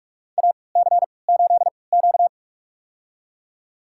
Morse code, heard as a keyed tone, about 42 wpm, spaced as for 25 wpm, 700 Hz. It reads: AC8Q
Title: USA Call Signs - 4 Characters - 3x Character Spacing - 45wpm